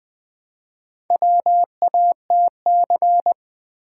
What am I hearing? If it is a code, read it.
Morse code, heard as a keyed tone, about 20 wpm, 700 Hz. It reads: WATC
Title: Build A Word - 1 Letter At A Time - 5-Letter Words - 20wpm